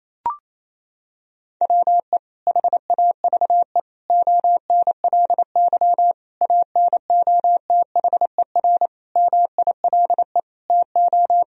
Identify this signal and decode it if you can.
Morse code, heard as a keyed tone, about 28 wpm, 700 Hz. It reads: WE HAVE ONLY ANOTHER MILE TO
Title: Sentences from Top 1000 Words - Review Part 1 - 28wpm